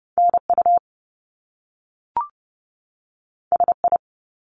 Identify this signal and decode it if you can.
Morse code, heard as a keyed tone, about 30 wpm, 700 Hz. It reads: NU SI
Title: Top Two Letter Combinations in Words 30wpm